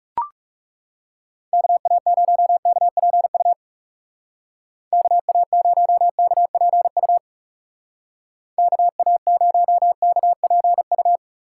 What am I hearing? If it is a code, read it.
Morse code, heard as a keyed tone, about 36 wpm, 700 Hz. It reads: KA0KPU KA0KPU KA0KPU
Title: Speed Racing - USA 2x3 Call Signs 30wpm